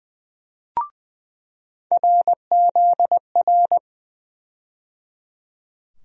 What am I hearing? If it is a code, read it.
Morse code, heard as a keyed tone, about 20 wpm, 700 Hz. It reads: RZR
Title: USA Call Sign Postfixes - 3 Letters 20wpm